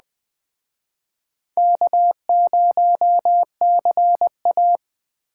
Morse code, heard as a keyed tone, about 21 wpm, 700 Hz. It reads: K0CA